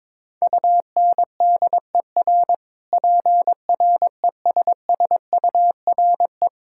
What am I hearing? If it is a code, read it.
Morse code, heard as a keyed tone, about 22 wpm, 700 Hz. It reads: UNDER PRESSURE
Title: Sentences from Top 1000 Words - Review Part 6 - 22wpm